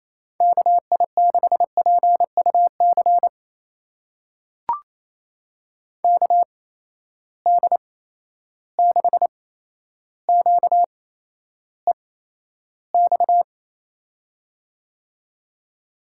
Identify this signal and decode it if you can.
Morse code, heard as a keyed tone, about 28 wpm, spaced as for 12 wpm, 700 Hz. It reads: KI6PUC KD6QEX
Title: USA Call Signs - 6 Characters - 8x Character Spacing - 28wpm